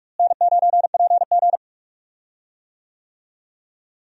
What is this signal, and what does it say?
Morse code, heard as a keyed tone, about 46 wpm, 700 Hz. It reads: N9PG
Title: USA Call Signs - 4 Characters - Repeated 2x - 45wpm